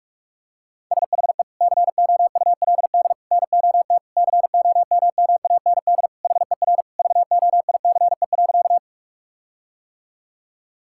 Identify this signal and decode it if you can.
Morse code, heard as a keyed tone, about 46 wpm, 700 Hz. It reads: SHE COULD NOT COMMAND HER VOICE.
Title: Sentences from Top 1600 Words 45wpm